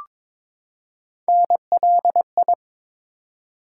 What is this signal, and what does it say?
Morse code, heard as a keyed tone, about 22 wpm, 700 Hz. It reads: NLI
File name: ARRL Sections v2 - 22wpm